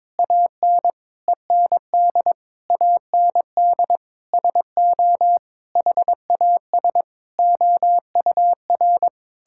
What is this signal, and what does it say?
Morse code, heard as a keyed tone, about 22 wpm, 700 Hz. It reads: AN END AND SO HAS OUR
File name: Sentences with 9-Word Phrases - 22wpm